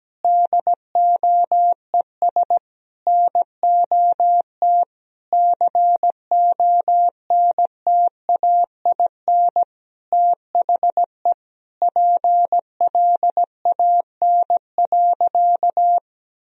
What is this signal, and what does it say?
Morse code, heard as a keyed tone, about 17 wpm, 700 Hz. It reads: DOES NOT CONTAIN THE PLAN.
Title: Sentences from Top 1000 Words - Review Part 2 - 17wpm